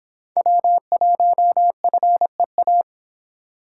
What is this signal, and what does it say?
Morse code, heard as a keyed tone, about 30 wpm, 700 Hz. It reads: W1FEA